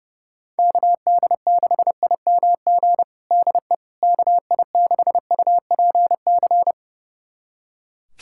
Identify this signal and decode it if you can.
Morse code, heard as a keyed tone, about 32 wpm, 700 Hz. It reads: KD6IMG DE KI6UPC
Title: USA Callsign DE Callsign 30wpm